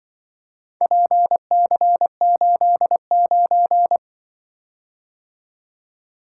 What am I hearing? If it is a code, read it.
Morse code, heard as a keyed tone, about 24 wpm, 700 Hz. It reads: PC89
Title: Speed-Racing - Four Letter-Number Combinations - 20wpm